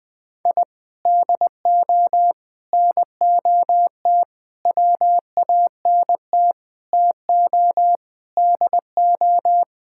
Morse code, heard as a keyed tone, about 20 wpm, 700 Hz. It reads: I DO NOT WANT TO DO
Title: Sentences with 5-Word Phrases - 20wpm